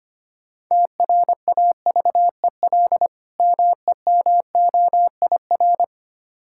Morse code, heard as a keyed tone, about 25 wpm, 690 Hz. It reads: TRAVEL MEMOIR
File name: practice / 25wpm / Difficult Words with Hints - 25wpm